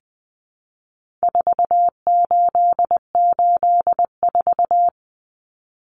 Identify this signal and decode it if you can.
Morse code, heard as a keyed tone, about 20 wpm, 700 Hz. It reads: 4884
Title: Common Cut Numbers - 4-Digit 20wpm